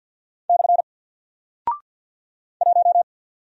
Morse code, heard as a keyed tone, about 48 wpm, 700 Hz. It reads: / 1
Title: Single Character (Letter:Number:Punctuation:Prosign) Alt 50wpm